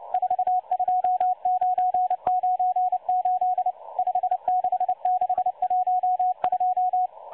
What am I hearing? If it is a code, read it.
Morse code, heard as a keyed tone, about 29 wpm, 730 Hz. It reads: 42998 56612
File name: M12-Morse-Numbers-Station-8116-kHz-CW